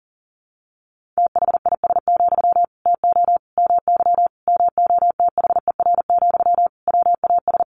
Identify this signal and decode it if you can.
Morse code, heard as a keyed tone, about 40 wpm, 700 Hz. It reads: THIS, TO MY MOTHER, WAS
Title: Sentences from Top 1500 Words 40wpm